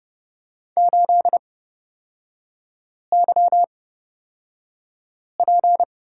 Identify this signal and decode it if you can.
Morse code, heard as a keyed tone, about 31 wpm, spaced as for 5 wpm, 690 Hz. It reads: 8YP
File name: Lesson 37e - Letter x - Three Characters 30wpm